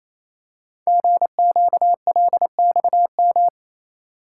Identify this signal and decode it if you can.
Morse code, heard as a keyed tone, about 27 wpm, 700 Hz. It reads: GQLXM
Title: Code Groups - 5 Letters - 28wpm